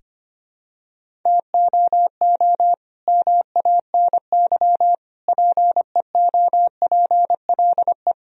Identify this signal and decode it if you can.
Morse code, heard as a keyed tone, about 25 wpm, 700 Hz. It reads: TOO MANY PEOPLE
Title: Sentences from Top 1000 Words - Review Part 1 - 25wpm